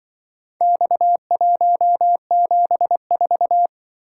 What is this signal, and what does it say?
Morse code, heard as a keyed tone, about 24 wpm, 700 Hz. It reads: X174
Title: Speed-Racing - Four Letter-Number Combinations - 20wpm